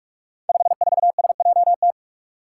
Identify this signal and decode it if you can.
Morse code, heard as a keyed tone, about 44 wpm, 700 Hz. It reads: 54SJT
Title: Five Challenging Letter-Number Combinations 45wpm